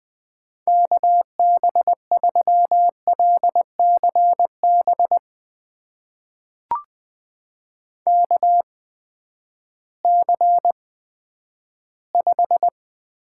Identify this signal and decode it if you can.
Morse code, heard as a keyed tone, about 20 wpm, spaced as for 11 wpm, 700 Hz. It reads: KB3LCB KC5
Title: USA Call Signs - 6 Characters - 8x Character Spacing - 20wpm